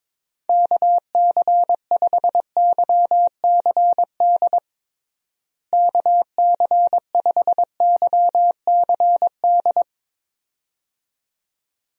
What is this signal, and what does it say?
Morse code, heard as a keyed tone, about 22 wpm, 700 Hz. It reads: KC5YCD KC5YCD
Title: USA Call Signs - 6 Characters - Repeated 2x - 22wpm